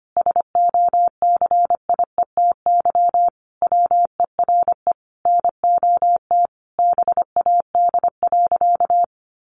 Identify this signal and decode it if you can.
Morse code, heard as a keyed tone, about 25 wpm, 700 Hz. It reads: SOCIETY WERE NOT BAD.
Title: Sentences from Top 900 Words 25wpm